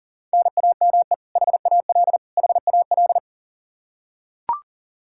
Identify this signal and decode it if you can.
Morse code, heard as a keyed tone, about 40 wpm, 700 Hz. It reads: NAME HAL HAL
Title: Practice QSO - Exchange 1 - Beginning - 40wpm